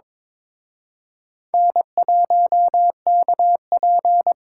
Morse code, heard as a keyed tone, about 26 wpm, 700 Hz. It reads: N1KP
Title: Speed Racing - USA 1x2 Call Signs 22wpm